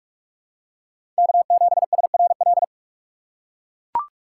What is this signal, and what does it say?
Morse code, heard as a keyed tone, about 43 wpm, 700 Hz. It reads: K7SRL
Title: USA Call Signs - 4x Character Spacing - 45wpm